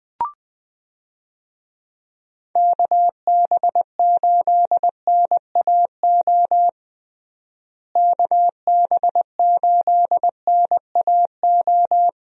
Morse code, heard as a keyed tone, about 20 wpm, 700 Hz. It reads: KB8NAO KB8NAO
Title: USA Call Signs - 6 Characters - Repeated 2x - 20wpm